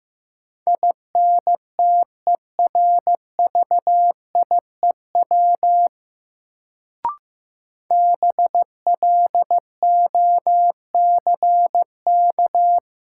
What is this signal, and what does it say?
Morse code, heard as a keyed tone, about 15 wpm, 700 Hz. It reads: INTERVIEW BLOCK
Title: Sentences from Top 1600 Words - New Word Drill - 15wpm